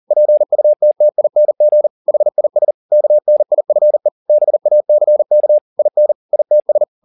Morse code, heard as a keyed tone, about 40 wpm, 600 Hz. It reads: PUTTING HIS KNIFE BACK IN ITS
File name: the old man and the sea 2 40wpm